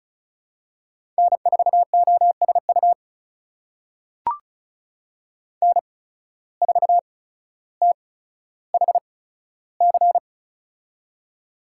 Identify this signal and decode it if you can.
Morse code, heard as a keyed tone, about 36 wpm, spaced as for 15 wpm, 700 Hz. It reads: N4OSU N4THC
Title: USA Call Signs - 5 Characters - 8x Character Spacing - 35wpm